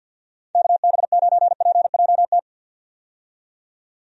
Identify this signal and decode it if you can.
Morse code, heard as a keyed tone, about 50 wpm, 700 Hz. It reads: KB9PJT
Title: USA Call Signs - 6 Characters - 50wpm